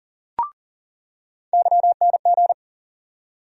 Morse code, heard as a keyed tone, about 40 wpm, 700 Hz. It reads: YNG
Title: USA Call Sign Postfixes - 3 Letters 40wpm